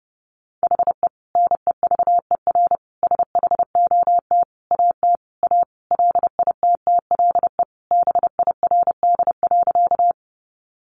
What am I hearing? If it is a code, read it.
Morse code, heard as a keyed tone, about 30 wpm, 700 Hz. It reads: HE NEVER SHOT AT A LITTLE BIRD.
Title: Sentences from Top 1100 Words 30wpm